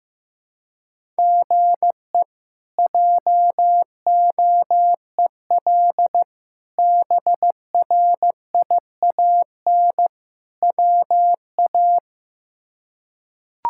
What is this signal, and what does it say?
Morse code, heard as a keyed tone, about 15 wpm, 700 Hz. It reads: GE JOEL BRIAN WA
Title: K1USN SST QSO Practice - 15wpm